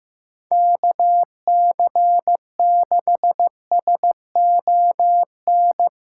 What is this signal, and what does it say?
Morse code, heard as a keyed tone, about 16 wpm, 700 Hz. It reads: KC6SON
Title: USA Callsign DE Callsign 15wpm